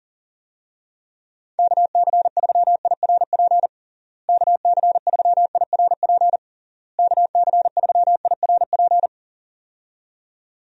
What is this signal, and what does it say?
Morse code, heard as a keyed tone, about 40 wpm, 700 Hz. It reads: KC3IRP KC3IRP KC3IRP
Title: USA Call Signs - 6 Characters - Repeated 3x - 40wpm